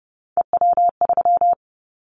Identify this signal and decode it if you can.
Morse code, heard as a keyed tone, about 32 wpm, 700 Hz. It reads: EW3
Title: Callsign Prefixes for Top 50 DXCC - 30wpm